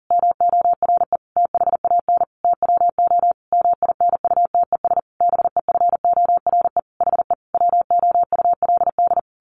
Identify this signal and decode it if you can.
Morse code, heard as a keyed tone, about 40 wpm, 700 Hz. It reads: MORE THAN TWO MINUTES BEFORE HE WOULD
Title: A Sound Of Thunder 40wpm